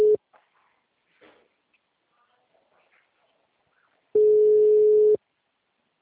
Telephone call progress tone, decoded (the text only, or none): ringback tone